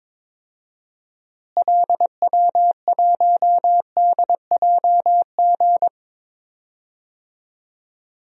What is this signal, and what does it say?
Morse code, heard as a keyed tone, about 22 wpm, 700 Hz. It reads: LW1DJG
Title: DXCC-100 Call Signs - 6 Characters - Repeated 2x - 22wpm